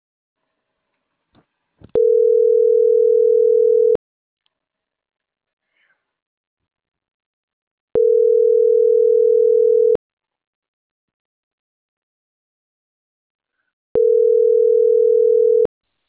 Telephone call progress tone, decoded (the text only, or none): ringback tone